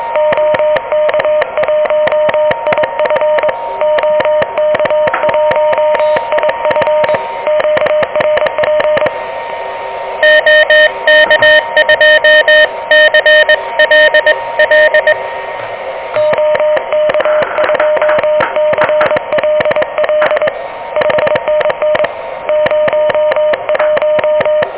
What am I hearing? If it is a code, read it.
Morse code, heard as a keyed tone, about 22 wpm, 600 Hz, with noise. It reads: OK1IF OK1IF QRP OK2CLL OK2CLL 5NN 01
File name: OK QRP 2023 / OK QRP 2023